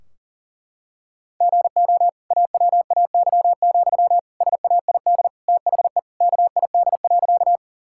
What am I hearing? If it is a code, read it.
Morse code, heard as a keyed tone, about 37 wpm, 700 Hz. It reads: GO AWAY, SAID THE KID.
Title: Sentences from Top 1600 Words 40wpm